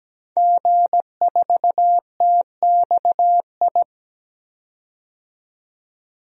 Morse code, heard as a keyed tone, about 17 wpm, 700 Hz. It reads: G4TXI
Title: DXCC-100 Call Signs - 5 Characters - 17wpm